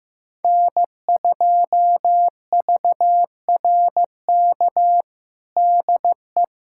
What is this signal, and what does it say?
Morse code, heard as a keyed tone, about 15 wpm, 700 Hz. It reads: N2VRK DE
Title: USA Callsign DE Callsign - 5 Characters - 15wpm